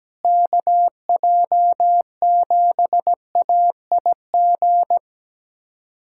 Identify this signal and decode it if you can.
Morse code, heard as a keyed tone, about 17 wpm, 700 Hz. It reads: KJ7AIG
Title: USA Callsign DE Callsign - 6 Characters - 17wpm